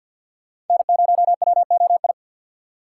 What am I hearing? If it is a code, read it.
Morse code, heard as a keyed tone, about 53 wpm, 690 Hz. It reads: N0WOI
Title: K1USN SST QSO Practice - 50wpm